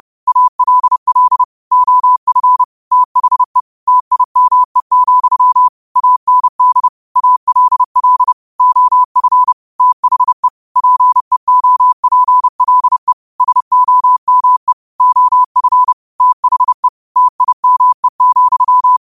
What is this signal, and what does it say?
Morse code, heard as a keyed tone, about 30 wpm, 1000 Hz. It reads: ALL OF THE TIME, AND ALL OF THE PEOPLE SOME OF THE TIME,
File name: qotdfiles / 30wpm